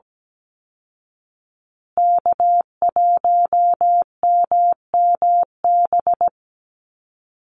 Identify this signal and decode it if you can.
Morse code, heard as a keyed tone, about 17 wpm, 700 Hz. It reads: K1MMB
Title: Call Signs Repeated 2x - 17wpm